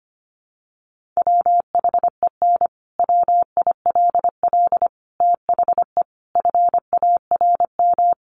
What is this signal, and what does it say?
Morse code, heard as a keyed tone, about 25 wpm, 700 Hz. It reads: WHEN WILL THE FARM
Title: Sentences from Top 1400 Words 25wpm